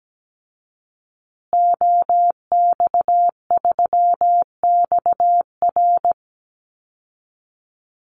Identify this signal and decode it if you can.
Morse code, heard as a keyed tone, about 17 wpm, 700 Hz. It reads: OX3XR
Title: Call Signs Repeated 3x - 17wpm